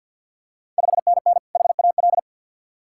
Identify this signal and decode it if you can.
Morse code, heard as a keyed tone, about 53 wpm, 700 Hz. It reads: 5NN HAL
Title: Ontario QSO Party Exchange 50wpm